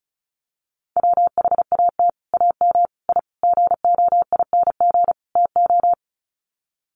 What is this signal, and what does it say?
Morse code, heard as a keyed tone, about 34 wpm, 700 Hz. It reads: WHAT AM I GOING TO
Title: Sets of 5 Words Encore 35wpm